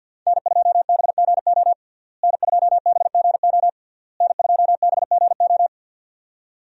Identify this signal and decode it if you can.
Morse code, heard as a keyed tone, about 46 wpm, 700 Hz. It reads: N2BGO N2BGO N2BGO
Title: USA Call Signs - 5 Characters - Repeated 3x - 50wpm